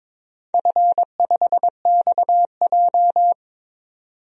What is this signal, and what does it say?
Morse code, heard as a keyed tone, about 21 wpm, 700 Hz. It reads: F5XJ
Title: DXCC-100 Call Signs - 4 Characters - 22wpm